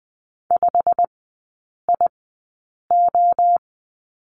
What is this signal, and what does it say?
Morse code, heard as a keyed tone, about 20 wpm, spaced as for 8 wpm, 690 Hz. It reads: 5IO
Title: 159 Lesson 26e - Letter g - Three Characters 20wpm